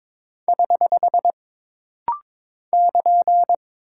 Morse code, heard as a keyed tone, about 22 wpm, 700 Hz. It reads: <HH> <KN>
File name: practice / 22wpm / Single Punctuation-Prosign - Rapid-Fire - 22wpm